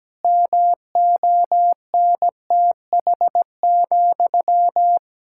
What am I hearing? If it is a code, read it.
Morse code, heard as a keyed tone, about 17 wpm, 700 Hz. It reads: MONTH,
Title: Sentences from Top 1000 Words - Review Part 5 - 17wpm